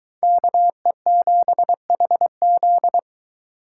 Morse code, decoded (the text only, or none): KE7HZ